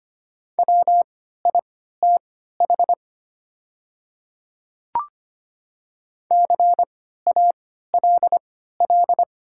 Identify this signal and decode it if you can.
Morse code, heard as a keyed tone, about 25 wpm, spaced as for 14 wpm, 700 Hz. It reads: WITH CALL
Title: 3x Character Spacing - Top 100 Words - 25wpm